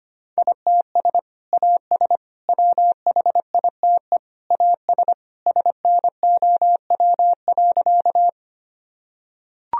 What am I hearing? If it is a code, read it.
Morse code, heard as a keyed tone, about 25 wpm, 700 Hz. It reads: ITS AS WHITE AS SNOW.
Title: Sentences from Top 1000 Words - Review Part 3 - 25wpm